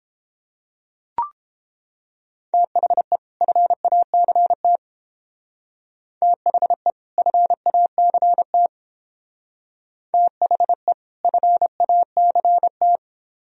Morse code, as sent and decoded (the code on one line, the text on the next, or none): - .... / ..-. .- -.-. - / - .... / ..-. .- -.-. - / - .... / ..-. .- -.-. -
THE FACT THE FACT THE FACT